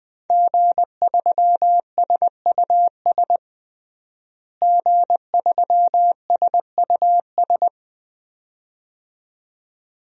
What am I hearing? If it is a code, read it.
Morse code, heard as a keyed tone, about 20 wpm, 700 Hz. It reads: G3SUS G3SUS
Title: DXCC-100 Call Signs - 5 Characters - Repeated 2x - 20wpm